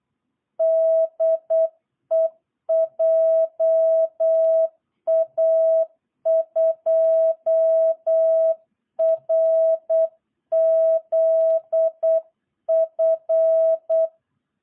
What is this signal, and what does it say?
Morse code, heard as a keyed tone, about 8 wpm, 660 Hz. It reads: DE JA2RZF
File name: jh8cbh / 2022cw